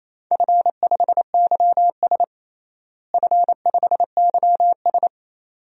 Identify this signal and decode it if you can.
Morse code, heard as a keyed tone, about 28 wpm, 700 Hz. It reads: F5YS F5YS
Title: DXCC-100 Call Signs - 4 Characters - Repeated 2x - 28wpm